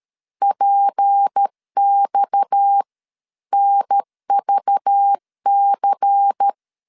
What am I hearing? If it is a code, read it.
Morse code, heard as a keyed tone, about 13 wpm, 800 Hz. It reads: PX NVC
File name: LEZIONI / 8 YJ45 / 8 YJ45 12wpm 2151